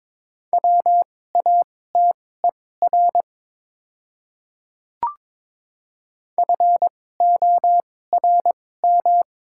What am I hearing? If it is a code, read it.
Morse code, heard as a keyed tone, about 23 wpm, spaced as for 16 wpm, 700 Hz. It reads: WATER FORM
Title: 2x Character Spacing - Top 300 Words - 22wpm